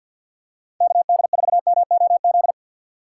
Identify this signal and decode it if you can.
Morse code, heard as a keyed tone, about 47 wpm, 700 Hz. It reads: KD4MOZ
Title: USA Call Signs - 6 Characters - 6x Character Spacing - 50wpm